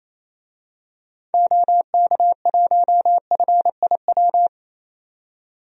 Morse code, heard as a keyed tone, about 28 wpm, 700 Hz. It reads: OK1FIW
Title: DXCC-100 Call Signs - 6 Characters - 28wpm